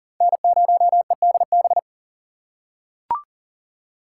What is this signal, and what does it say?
Morse code, heard as a keyed tone, about 40 wpm, 700 Hz. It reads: N0EDB
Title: USA Call Signs - 5 Characters - 4x Character Spacing - 40wpm